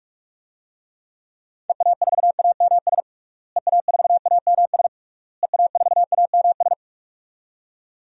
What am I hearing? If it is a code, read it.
Morse code, heard as a keyed tone, about 45 wpm, 700 Hz. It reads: EA4AMS EA4AMS EA4AMS
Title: DXCC-100 Call Signs - 6 Characters - Repeated 3x - 45wpm